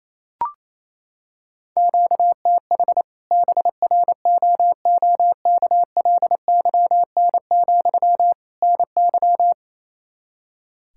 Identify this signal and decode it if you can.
Morse code, heard as a keyed tone, about 28 wpm, 700 Hz. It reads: QTH BROOKLYN, NY
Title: Practice QSO - Exchange 1 - Beginning - 28wpm